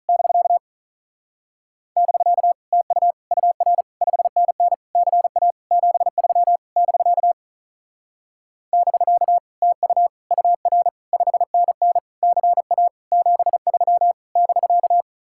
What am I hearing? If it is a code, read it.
Morse code, heard as a keyed tone, about 37 wpm, 700 Hz. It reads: <BK> <BK> TU UR 5NN CA 73 <BK> <BK> TU UR 5NN CA 73 <BK>
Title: Speed-Racing Common QSO Phrases Encore - 35wpm